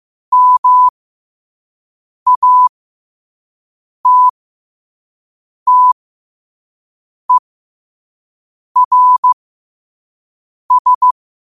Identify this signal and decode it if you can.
Morse code, heard as a keyed tone, about 15 wpm, spaced as for 5 wpm, 1000 Hz. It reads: MATTERS
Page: New quotes every day in morse code at 5 Words per minute.